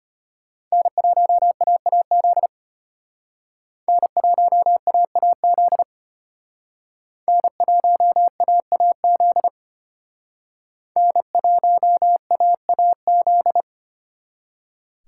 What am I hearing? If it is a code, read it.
Morse code, heard as a keyed tone, about 33 wpm, 700 Hz. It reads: N1AAZ N1AAZ N1AAZ N1AAZ